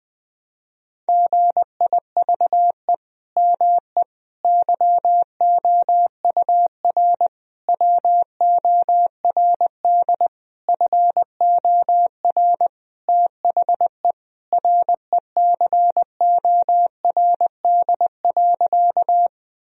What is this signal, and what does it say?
Morse code, heard as a keyed tone, about 21 wpm, 700 Hz. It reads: GIVE ME YOUR WORD FOR THE RECORD.
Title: Sentences from Top 1000 Words - Review Part 3 - 20wpm